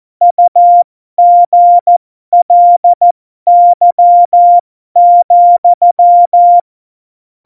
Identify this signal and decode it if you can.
Morse code, heard as a keyed tone, about 14 wpm, spaced as for 12 wpm, 700 Hz. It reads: UGLY,
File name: Morsekurs / Witze